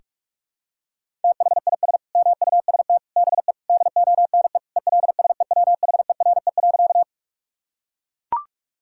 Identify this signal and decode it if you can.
Morse code, heard as a keyed tone, about 43 wpm, 700 Hz. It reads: THIS MUST BE DONE ELSEWHERE.
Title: Sentences from Top 1600 Words 45wpm